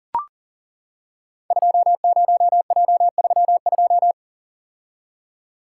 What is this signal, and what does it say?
Morse code, heard as a keyed tone, about 41 wpm, 700 Hz. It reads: 20J32
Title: Code Groups - 5 Letter:Numbers - 40wpm